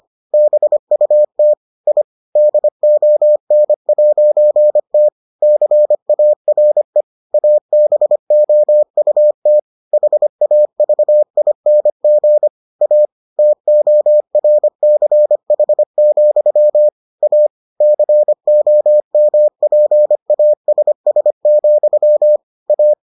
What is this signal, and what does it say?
Morse code, heard as a keyed tone, about 25 wpm, 600 Hz. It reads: BUT I DON'T CARE ABOUT HAVING A TORCH, A COMPASS, A